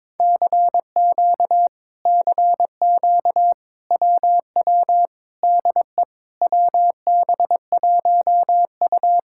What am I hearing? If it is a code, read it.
Morse code, heard as a keyed tone, about 22 wpm, 700 Hz. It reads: CQ CQ WW DE WB1U
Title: Speed-Racing Common QSO Phrases - 22wpm